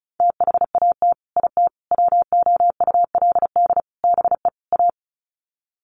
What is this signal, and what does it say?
Morse code, heard as a keyed tone, about 34 wpm, 700 Hz. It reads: THAT IT WOULD BE A
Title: Sets of 5 Words Encore2 35wpm